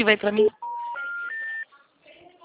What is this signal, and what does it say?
Special information tone, with speech.